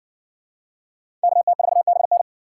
Morse code, heard as a keyed tone, about 60 wpm, 700 Hz. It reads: KT4BN